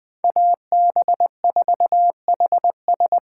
Morse code, heard as a keyed tone, about 20 wpm, 700 Hz. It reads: AB4HS
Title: Speed Racing - USA 2x2 Call Signs 17wpm